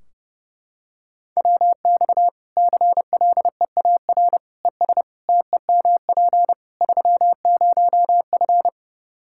Morse code, heard as a keyed tone, about 30 wpm, 700 Hz. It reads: WX CLEAR ES TEMP 30F
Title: Practice QSO - Exchange 2 - Info - 30wpm